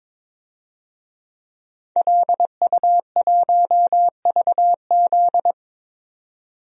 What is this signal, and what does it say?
Morse code, heard as a keyed tone, about 22 wpm, 700 Hz. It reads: LU1VZ
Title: DXCC-100 Call Signs - 5 Characters - 22wpm